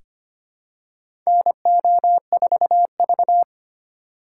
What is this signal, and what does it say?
Morse code, heard as a keyed tone, about 25 wpm, 700 Hz. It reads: NO4V
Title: USA Call Signs - 4 Characters - 4x Character Spacing - 25wpm